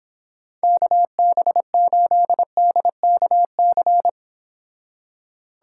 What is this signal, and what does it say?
Morse code, heard as a keyed tone, about 29 wpm, 700 Hz. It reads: KB8DKC